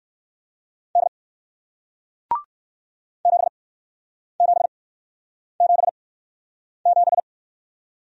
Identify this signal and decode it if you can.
Morse code, heard as a keyed tone, about 56 wpm, spaced as for 9 wpm, 700 Hz. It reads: D 7777
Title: Speed-Racing Single Letter-Number 45wpm